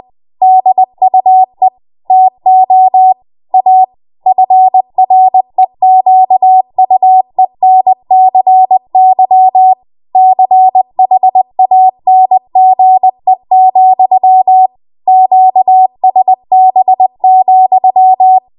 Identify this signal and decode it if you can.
Morse code, heard as a keyed tone, about 20 wpm, 750 Hz. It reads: DUE TO A FREQUENCY CHANGE, QSB,
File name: Morse / 20WPM